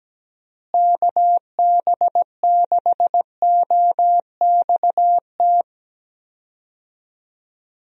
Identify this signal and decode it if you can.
Morse code, heard as a keyed tone, about 17 wpm, 700 Hz. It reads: KB6OXT
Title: USA Call Signs - 6 Characters - Repeated 2x - 17wpm